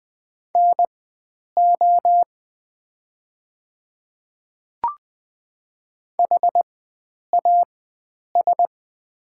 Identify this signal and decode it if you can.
Morse code, heard as a keyed tone, about 20 wpm, spaced as for 9 wpm, 700 Hz. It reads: NO HAS
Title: 4x Character Spacing - Top 100 Words - 20wpm